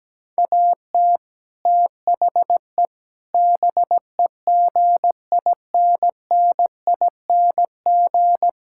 Morse code, heard as a keyed tone, about 17 wpm, 700 Hz. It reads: AT THE BEGINNING
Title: Sentences from Top 1000 Words - Review Part 5 - 17wpm